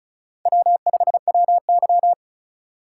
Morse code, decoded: W5WY

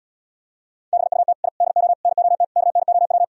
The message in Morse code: BUT I DO NOT KNOW

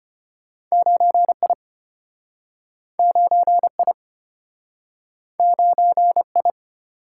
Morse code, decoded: 9I 9I 9I

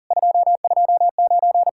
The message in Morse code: 229

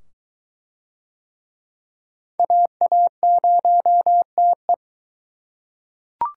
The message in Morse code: AA0TE

